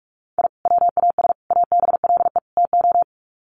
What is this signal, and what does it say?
Morse code, heard as a keyed tone, about 44 wpm, 700 Hz. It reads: I WAS ABLE TO
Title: Sets of 4 Words 45wpm